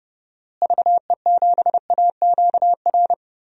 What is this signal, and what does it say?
Morse code, heard as a keyed tone, about 30 wpm, 700 Hz. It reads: VE7AQR
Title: DXCC-100 Call Signs - 6 Characters - Repeated 2x - 30wpm